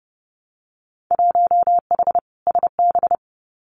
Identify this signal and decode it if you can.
Morse code, heard as a keyed tone, about 30 wpm, 700 Hz. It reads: 1H SB
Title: Winter Field Day Exchanges 30wpm